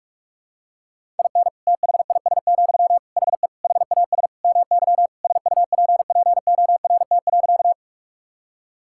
Morse code, heard as a keyed tone, about 46 wpm, 700 Hz. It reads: IN THIS, HE HAS MY SUPPORT.
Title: Sentences from Top 1000 Words - Review Part 5 - 45wpm